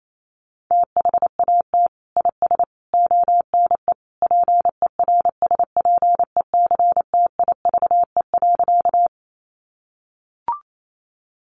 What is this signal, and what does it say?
Morse code, heard as a keyed tone, about 28 wpm, 700 Hz. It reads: THAT IS ONE PERSPECTIVE.
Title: Sentences from Top 1200 Words 28wpm